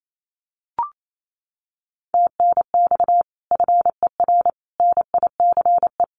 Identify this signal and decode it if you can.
Morse code, heard as a keyed tone, about 28 wpm, 700 Hz. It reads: TNX FER NICE
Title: Common QSO Phrases Encore 28wpm